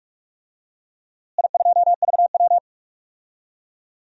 Morse code, heard as a keyed tone, about 47 wpm, 700 Hz. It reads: I2VW
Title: DXCC-100 Call Signs - 4 Characters - Repeated 3x - 45wpm